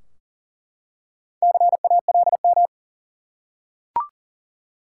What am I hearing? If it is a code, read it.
Morse code, heard as a keyed tone, about 40 wpm, 700 Hz. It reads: CALM
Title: Four Letter Words - 40wpm